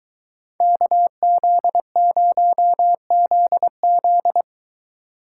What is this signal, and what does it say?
Morse code, heard as a keyed tone, about 21 wpm, 700 Hz. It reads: KZ0ZZ